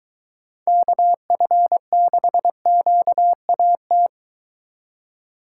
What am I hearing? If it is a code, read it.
Morse code, heard as a keyed tone, about 23 wpm, 700 Hz. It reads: KF6QAT